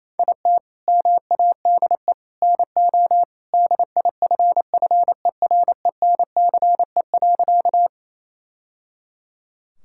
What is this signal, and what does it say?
Morse code, heard as a keyed tone, about 28 wpm, 700 Hz. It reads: IT MADE NO DIFFERENCE.
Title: Sentences from Top 1000 Words - Review Part 5 - 28wpm